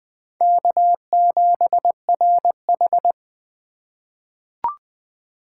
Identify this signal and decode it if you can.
Morse code, heard as a keyed tone, about 20 wpm, 700 Hz. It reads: K7RH
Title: USA Call Signs - 4 Characters - 2x Character Spacing - 20wpm